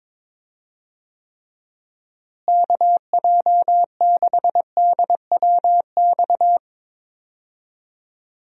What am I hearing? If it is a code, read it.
Morse code, heard as a keyed tone, about 21 wpm, 700 Hz. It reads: KJ6DWX
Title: USA Call Signs - 6 Characters - Repeated 3x - 22wpm